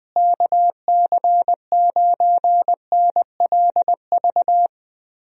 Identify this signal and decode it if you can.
Morse code, heard as a keyed tone, about 20 wpm, 700 Hz. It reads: KC9NLV
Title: USA Call Signs - 6 Characters - Repeated 3x - 20wpm